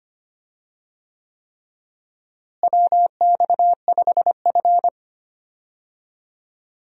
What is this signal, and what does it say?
Morse code, heard as a keyed tone, about 25 wpm, 700 Hz. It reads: WX5F
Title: USA Call Signs - 4 Characters - Repeated 3x - 25wpm